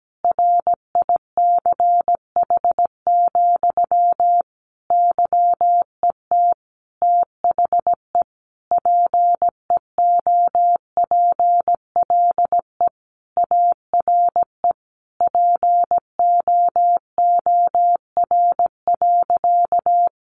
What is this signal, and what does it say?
Morse code, heard as a keyed tone, about 17 wpm, 700 Hz. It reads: RICH, YET THE PEOPLE ARE POOR.
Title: Sentences from Top 1100 Words 17wpm